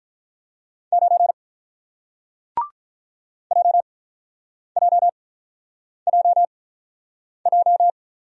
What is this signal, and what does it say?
Morse code, heard as a keyed tone, about 48 wpm, spaced as for 9 wpm, 700 Hz. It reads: 9 JJJJ